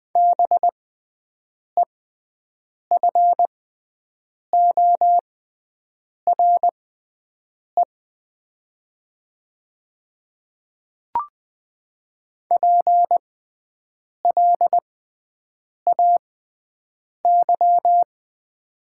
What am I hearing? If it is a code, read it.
Morse code, heard as a keyed tone, about 20 wpm, spaced as for 7 wpm, 700 Hz. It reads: BEFORE PLAY
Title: 6x Character Spacing - Top 200 Words - 20wpm